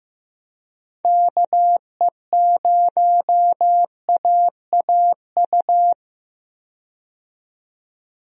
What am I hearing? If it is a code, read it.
Morse code, heard as a keyed tone, about 15 wpm, 700 Hz. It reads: KE0AAU
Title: USA Call Signs - 6 Characters - 15wpm